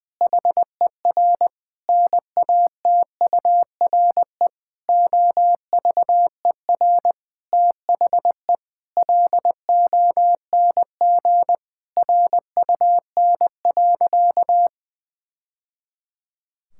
Morse code, heard as a keyed tone, about 21 wpm, 700 Hz. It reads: HER NATURE OVER THE LONG RUN.
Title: Sentences from Top 1000 Words - Review Part 6 - 20wpm